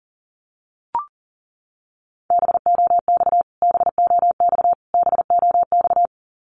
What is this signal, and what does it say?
Morse code, heard as a keyed tone, about 37 wpm, 700 Hz. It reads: BOX BOX BOX
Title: Top 500 Words Repeated 40wpm